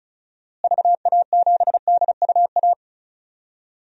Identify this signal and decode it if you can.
Morse code, heard as a keyed tone, about 36 wpm, 700 Hz. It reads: VA7DUA
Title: DXCC-100 Call Signs - 6 Characters - Repeated 2x - 35wpm